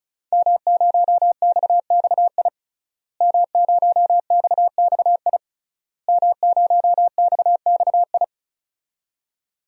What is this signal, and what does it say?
Morse code, heard as a keyed tone, about 36 wpm, 700 Hz. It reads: M0XXI M0XXI M0XXI
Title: DXCC-100 Call Signs - 5 Characters - Repeated 3x - 35wpm